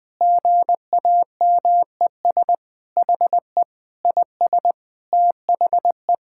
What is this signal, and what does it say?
Morse code, heard as a keyed tone, about 20 wpm, 700 Hz. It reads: GAMES HE IS THE
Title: Sentences with 3-Word Phrases - 20wpm